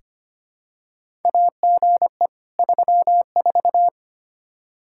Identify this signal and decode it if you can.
Morse code, heard as a keyed tone, about 25 wpm, 700 Hz. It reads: AGE 34
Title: Practice QSO - Exchange 3 - More Info - 25wpm